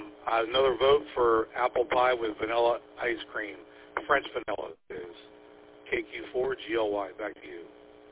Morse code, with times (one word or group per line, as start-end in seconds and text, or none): none